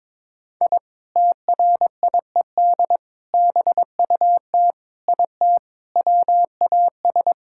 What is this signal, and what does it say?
Morse code, heard as a keyed tone, about 23 wpm, 700 Hz. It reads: I TRIED BUT IT WAS
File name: practice / 22wpm / Sentences with 3-Word Phrases - 22wpm